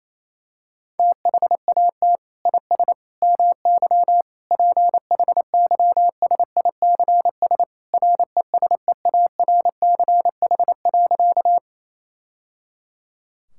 Morse code, heard as a keyed tone, about 28 wpm, 700 Hz. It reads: THAT IS MY PHYSICS RESEARCH.
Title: Sentences from Top 1000 Words - Review Part 5 - 28wpm